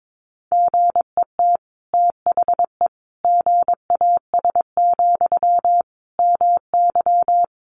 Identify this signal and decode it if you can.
Morse code, heard as a keyed tone, about 22 wpm, 700 Hz. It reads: GET THE GAS, MY
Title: Sentences from Top 1100 Words 22wpm